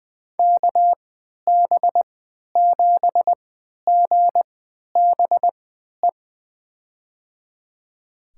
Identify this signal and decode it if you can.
Morse code, heard as a keyed tone, about 20 wpm, spaced as for 11 wpm, 700 Hz. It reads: KB7GBE
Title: USA Call Signs - 6 Characters - 3x Character Spacing - 20wpm